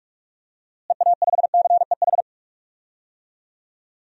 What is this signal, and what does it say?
Morse code, heard as a keyed tone, about 46 wpm, 700 Hz. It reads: EA5CEH
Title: DXCC-100 Call Signs - 6 Characters - Repeated 2x - 45wpm